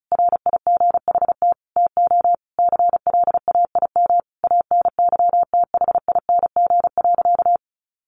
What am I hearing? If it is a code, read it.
Morse code, heard as a keyed tone, about 34 wpm, 700 Hz. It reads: RIGHT TO CLAIM ANYTHING.
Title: Sentences from Top 1500 Words 35wpm